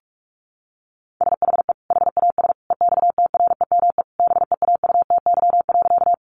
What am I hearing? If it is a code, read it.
Morse code, heard as a keyed tone, about 44 wpm, 700 Hz. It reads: SHE HAS EXTREME BEAUTY.
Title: Sentences from Top 1300 Words 45wpm